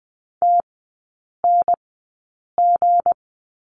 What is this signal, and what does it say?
Morse code, heard as a keyed tone, about 20 wpm, spaced as for 8 wpm, 700 Hz. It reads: TNG
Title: 184 Lesson 30e - Letter b - Three Characters 20wpm